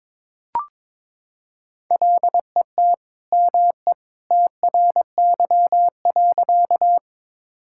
Morse code, heard as a keyed tone, about 22 wpm, 700 Hz. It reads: LET ME TRY.
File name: practice / 22wpm / Sentences with 2-Word Phrases - 22wpm